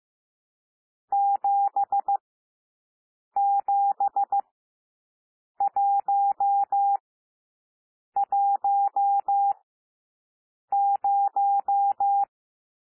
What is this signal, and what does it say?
Morse code, heard as a keyed tone, about 15 wpm, spaced as for 6 wpm, 800 Hz. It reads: 77110
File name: cwmp3 / N1 SOUND NUMBERS 15 wpm